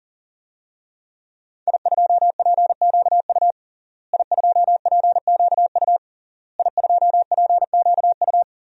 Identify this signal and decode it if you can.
Morse code, heard as a keyed tone, about 40 wpm, 700 Hz. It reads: I2PQU I2PQU I2PQU
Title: DXCC-100 Call Signs - 5 Characters - Repeated 3x - 40wpm